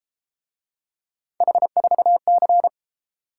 Morse code, heard as a keyed tone, about 31 wpm, 700 Hz. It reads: H4C